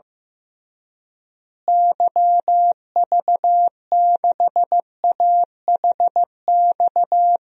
Morse code, heard as a keyed tone, about 15 wpm, 700 Hz. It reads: YV6AHX
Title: DXCC-100 Call Signs - 6 Characters - Repeated 3x - 15wpm